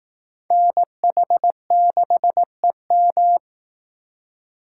Morse code, heard as a keyed tone, about 19 wpm, 700 Hz. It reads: NH6EM